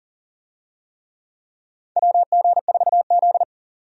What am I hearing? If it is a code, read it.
Morse code, heard as a keyed tone, about 38 wpm, 700 Hz. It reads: WG4Z
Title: USA Call Signs - 4 Characters - 40wpm